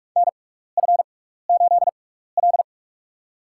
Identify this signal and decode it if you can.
Morse code, heard as a keyed tone, about 46 wpm, spaced as for 15 wpm, 700 Hz. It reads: NF8L
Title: USA Call Signs - 6x Character Spacing - 45wpm